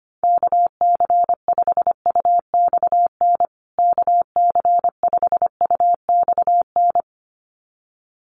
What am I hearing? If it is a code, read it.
Morse code, heard as a keyed tone, about 25 wpm, 700 Hz. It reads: KC5UXN KC5UXN
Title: USA Call Signs Repeated 2x 25wpm